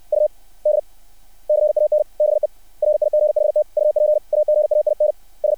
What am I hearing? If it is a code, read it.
Morse code, heard as a keyed tone, about 60 wpm, 600 Hz. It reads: U D CAN BE RIGHT UP AGAIN S
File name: archived / CRSnet